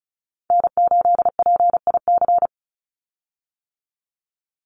Morse code, heard as a keyed tone, about 34 wpm, 700 Hz. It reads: N8PIC
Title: USA Call Signs 35wpm